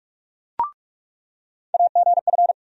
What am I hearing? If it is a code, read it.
Morse code, heard as a keyed tone, about 46 wpm, 700 Hz. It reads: AGF
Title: USA Call Sign Postfixes - 3 Letters 45wpm